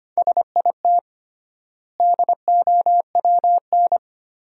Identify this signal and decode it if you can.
Morse code, heard as a keyed tone, about 25 wpm, 700 Hz. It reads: SIT DOWN
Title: 3x Word-Spacing - Sentences from Top 500 Words - 25wpm